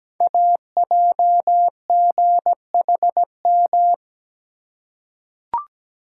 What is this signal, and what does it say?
Morse code, heard as a keyed tone, about 17 wpm, 700 Hz. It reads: AJGHM
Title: Code Groups - 5 Letters - 17wpm